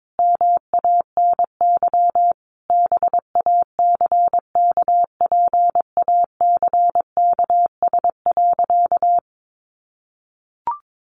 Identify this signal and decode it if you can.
Morse code, heard as a keyed tone, about 22 wpm, 700 Hz. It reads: MANY BACKPACKS.
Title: Additional Easy Phrases 22wpm